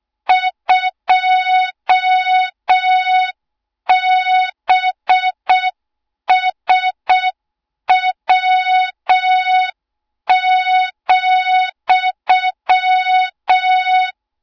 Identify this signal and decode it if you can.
Morse code, heard as a keyed tone, about 6 wpm, 3700 Hz, with noise. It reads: 2BSW,
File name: Morze-HA2MN / hang / 20 GYAKORLAT